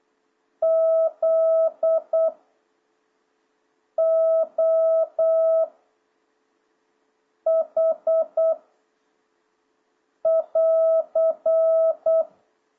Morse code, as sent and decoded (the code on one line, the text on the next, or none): --.. / --- / .... / .-.-.
Z O H <AR>